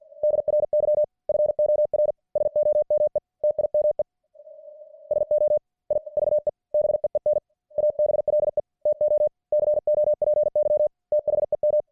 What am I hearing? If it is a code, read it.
Morse code, heard as a keyed tone, about 50 wpm, 600 Hz. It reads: DRY FOR SOME TIME SO I VE BEEN ABLE TO COPY THEM